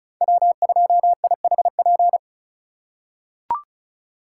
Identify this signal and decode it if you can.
Morse code, heard as a keyed tone, about 36 wpm, 700 Hz. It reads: W2IHP
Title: USA Call Signs - 3x Character Spacing - 35wpm